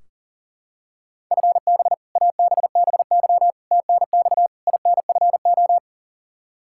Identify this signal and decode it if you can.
Morse code, heard as a keyed tone, about 40 wpm, 700 Hz. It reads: FB ABBY TNX INFO
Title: Practice QSO - Exchange 3 - More Info - 40wpm